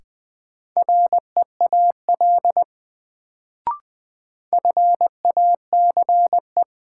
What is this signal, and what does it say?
Morse code, heard as a keyed tone, about 20 wpm, 700 Hz. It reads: REAL FACE
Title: Sentences from Top 300 Words - New Word Drill - 20wpm